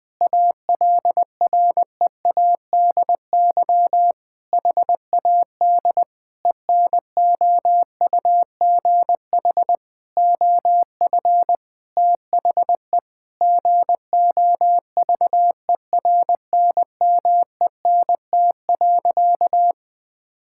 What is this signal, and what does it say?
Morse code, heard as a keyed tone, about 21 wpm, 700 Hz. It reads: ALREADY HAD ENOUGH OF THE GOVERNMENT.
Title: Sentences from Top 1600 Words 20wpm